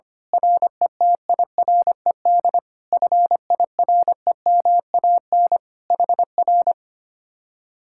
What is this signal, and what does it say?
Morse code, heard as a keyed tone, about 24 wpm, 690 Hz. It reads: RETIRED FIREMAN HR
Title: Speed-Racing Common QSO Phrases - 25wpm